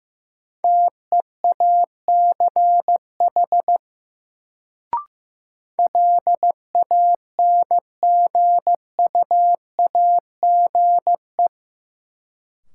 Morse code, heard as a keyed tone, about 15 wpm, 700 Hz. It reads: TEACH LANGUAGE
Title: Sentences from Top 500 Words - New Word Drill - 15wpm